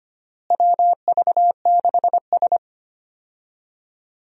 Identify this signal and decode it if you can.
Morse code, heard as a keyed tone, about 25 wpm, 700 Hz. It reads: WV6S
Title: USA Call Signs - 4 Characters - Repeated 3x - 25wpm